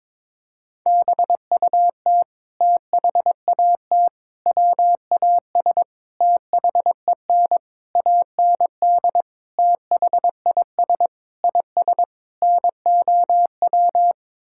Morse code, heard as a keyed tone, about 22 wpm, 700 Hz. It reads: BUT THAT WAS THEN AND THIS IS NOW
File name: practice / 22wpm / Sets of 8 Words - 22wpm